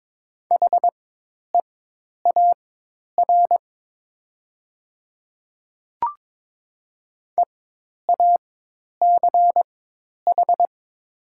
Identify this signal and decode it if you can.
Morse code, heard as a keyed tone, about 22 wpm, spaced as for 10 wpm, 700 Hz. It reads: HEAR EACH
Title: 4x Character Spacing - Top 500 Words - 22wpm